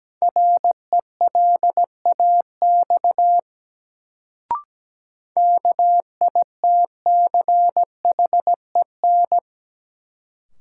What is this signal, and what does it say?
Morse code, heard as a keyed tone, about 17 wpm, 700 Hz. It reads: RELAX KITCHEN
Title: Sentences from Top 1400 Words - New Word Drill Alt - 17wpm